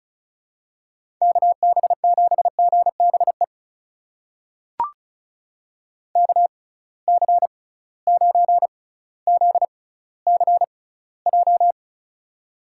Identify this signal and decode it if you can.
Morse code, heard as a keyed tone, about 36 wpm, spaced as for 18 wpm, 700 Hz. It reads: KB7GBE KC9ZCJ
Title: USA Call Signs - 6 Characters - 6x Character Spacing - 35wpm